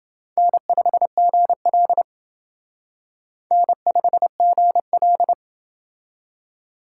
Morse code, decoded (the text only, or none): N5GL N5GL